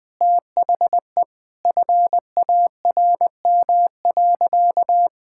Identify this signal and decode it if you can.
Morse code, heard as a keyed tone, about 20 wpm, 700 Hz. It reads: THE FARM.
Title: Sentences from Top 1000 Words - Review Part 5 - 20wpm